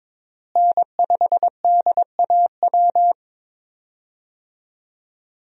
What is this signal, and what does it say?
Morse code, heard as a keyed tone, about 22 wpm, 700 Hz. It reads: N5DAW
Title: USA Call Signs - 5 Characters - Repeated 3x - 22wpm